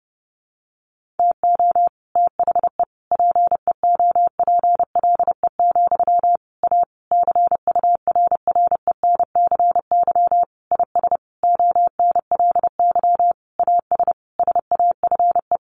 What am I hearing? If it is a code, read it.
Morse code, heard as a keyed tone, about 28 wpm, 700 Hz. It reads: TO THE PEOPLE, A CURRENCY IS ONLY AS SAFE
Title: Sentences from Top 1500 Words 30wpm